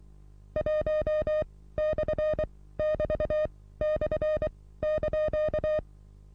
Morse code, heard as a keyed tone, about 23 wpm, spaced as for 16 wpm, 610 Hz. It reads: 1/<BT>/)
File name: repos / Morsekurs / DJ4UF Morsen27